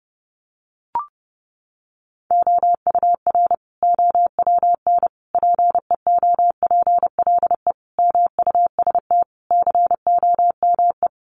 Morse code, heard as a keyed tone, about 30 wpm, 700 Hz. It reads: OUR OWN PEOPLE MUST COME
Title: Sentences from Top 300 Words 30wpm